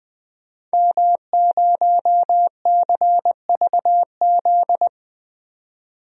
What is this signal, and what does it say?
Morse code, heard as a keyed tone, about 20 wpm, 700 Hz. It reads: M0CVZ
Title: DXCC-100 Call Signs - 5 Characters - 20wpm